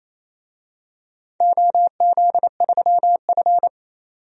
Morse code, heard as a keyed tone, about 28 wpm, 700 Hz. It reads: OZ3F
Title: DXCC-100 Call Signs - 4 Characters - 28wpm